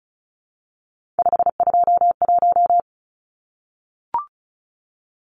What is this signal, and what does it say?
Morse code, heard as a keyed tone, about 34 wpm, 700 Hz. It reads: 521
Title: Numbers - 3-Digit 35wpm